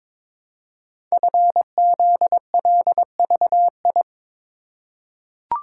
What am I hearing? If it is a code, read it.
Morse code, heard as a keyed tone, about 22 wpm, 700 Hz. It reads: FZLVI
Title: Code Groups - 5 Letters - 22wpm